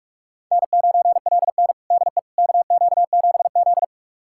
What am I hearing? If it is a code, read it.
Morse code, heard as a keyed tone, about 46 wpm, 700 Hz. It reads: N9LN DE KQ7Z
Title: USA Callsign DE Callsign - 4 Characters - 45wpm